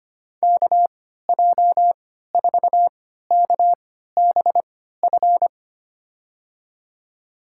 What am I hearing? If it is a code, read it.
Morse code, heard as a keyed tone, about 25 wpm, spaced as for 14 wpm, 700 Hz. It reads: KJ4KBF
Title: USA Call Signs - 3x Character Spacing - 25wpm